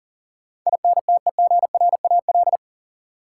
Morse code, heard as a keyed tone, about 40 wpm, 700 Hz. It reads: INTEGRAL
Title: Sets of 5 Words - New Word Drill - 40wpm